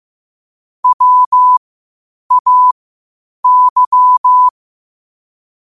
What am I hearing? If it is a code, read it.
Morse code, heard as a keyed tone, about 15 wpm, spaced as for 8 wpm, 1000 Hz. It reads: WAY
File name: qotdfiles / 8wpm